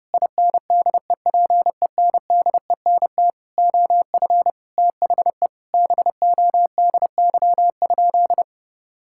Morse code, heard as a keyed tone, about 30 wpm, 700 Hz. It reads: INDEPENDENT OF THE BODY?
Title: Sentences from Top 1000 Words - Review Part 6 - 30wpm